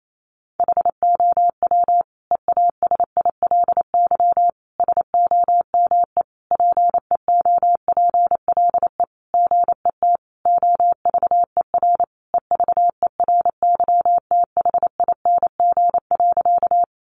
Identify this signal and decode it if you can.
Morse code, heard as a keyed tone, about 28 wpm, 700 Hz. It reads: HOW EASILY SOME PEOPLE GET OVER EVERYTHING.
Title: Sentences from Top 900 Words 28wpm